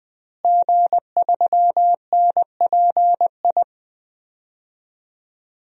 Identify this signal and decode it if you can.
Morse code, heard as a keyed tone, about 20 wpm, 700 Hz. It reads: G3NPI
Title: DXCC-100 Call Signs - 5 Characters - Repeated 2x - 20wpm